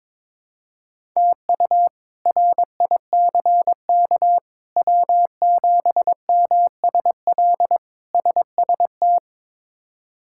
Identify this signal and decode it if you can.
Morse code, heard as a keyed tone, about 22 wpm, 700 Hz. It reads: TU RICK W7MSL SST
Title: K1USN SST QSO Practice - 22wpm